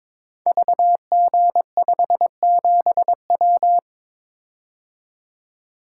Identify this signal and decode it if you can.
Morse code, heard as a keyed tone, about 23 wpm, 700 Hz. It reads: VG57W
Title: Code Groups - 5 Letter:Numbers - 22wpm